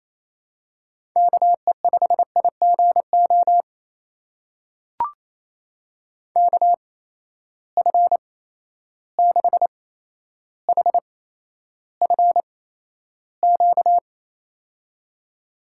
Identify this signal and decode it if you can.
Morse code, heard as a keyed tone, about 27 wpm, spaced as for 12 wpm, 700 Hz. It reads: KE5IGO KF6HFQ
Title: USA Call Signs - 6 Characters - 8x Character Spacing - 28wpm